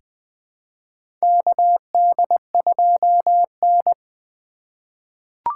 Morse code, heard as a keyed tone, about 20 wpm, 700 Hz. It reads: KD2N
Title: USA Call Signs - 4 Characters - 3x Character Spacing - 20wpm